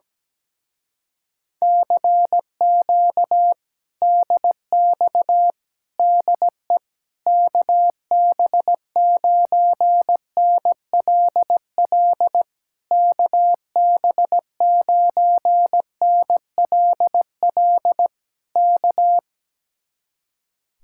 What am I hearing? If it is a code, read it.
Morse code, heard as a keyed tone, about 17 wpm, 700 Hz. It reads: CQ DX DE KB9NLL KB9NLL K
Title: Speed-Racing Common QSO Phrases - 17wpm